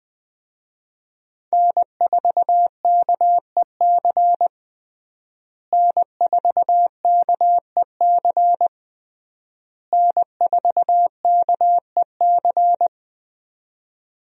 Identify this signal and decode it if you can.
Morse code, heard as a keyed tone, about 20 wpm, 700 Hz. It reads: N4KEC N4KEC N4KEC
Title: USA Call Signs - 5 Characters - Repeated 3x - 20wpm